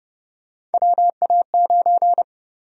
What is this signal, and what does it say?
Morse code, heard as a keyed tone, about 30 wpm, 700 Hz. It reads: WA9
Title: USA Call Sign Prefixes - 30wpm